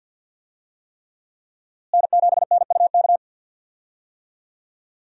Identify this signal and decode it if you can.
Morse code, heard as a keyed tone, about 52 wpm, 700 Hz. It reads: N7NUK
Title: USA Call Signs - 5 Characters - Repeated 3x - 50wpm